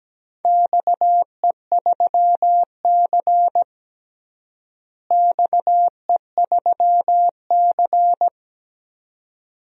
Morse code, heard as a keyed tone, about 17 wpm, 700 Hz. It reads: XE3C XE3C
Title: DXCC-100 Call Signs - 4 Characters - Repeated 2x - 17wpm